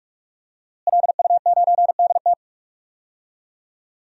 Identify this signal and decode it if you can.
Morse code, heard as a keyed tone, about 46 wpm, 700 Hz. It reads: LU9DT
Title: DXCC-100 Call Signs - 5 Characters - Repeated 2x - 45wpm